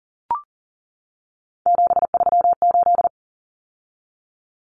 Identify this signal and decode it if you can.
Morse code, heard as a keyed tone, about 40 wpm, 700 Hz. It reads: 738
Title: Numbers - 3-Digit 40wpm